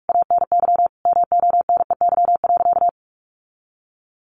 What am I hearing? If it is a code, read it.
Morse code, heard as a keyed tone, about 44 wpm, 700 Hz. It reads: ANY MONEY.
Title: Sentences from Top 1500 Words 45wpm